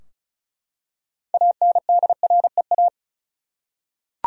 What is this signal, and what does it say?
Morse code, heard as a keyed tone, about 36 wpm, 700 Hz. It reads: ANDREA
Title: USA Common Names - 35wpm